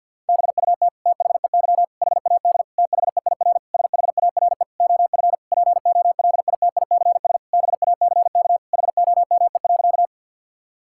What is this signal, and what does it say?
Morse code, heard as a keyed tone, about 50 wpm, 700 Hz. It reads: BUT THEY HAD THEIR SHARE OF POLITICS BACK HOME.
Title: Sentences from Top 1000 Words - Review Part 6 - 50wpm